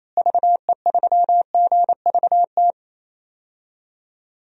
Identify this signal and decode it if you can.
Morse code, heard as a keyed tone, about 28 wpm, 700 Hz. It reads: VE3GVT
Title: DXCC-100 Call Signs - 6 Characters - Repeated 3x - 28wpm